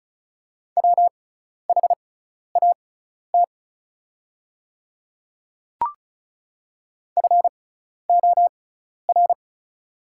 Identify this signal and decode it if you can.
Morse code, heard as a keyed tone, about 36 wpm, spaced as for 12 wpm, 700 Hz. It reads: WHAT FOR
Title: 6x Character Spacing - Top 100 Words - 35wpm